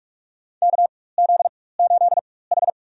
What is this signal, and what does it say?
Morse code, heard as a keyed tone, about 46 wpm, spaced as for 21 wpm, 700 Hz. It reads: KZ8H
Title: USA Call Signs - 4x Character Spacing - 45wpm